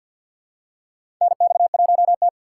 Morse code, heard as a keyed tone, about 53 wpm, 700 Hz. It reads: NX1T
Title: USA Call Signs - 2x Character Spacing - 50wpm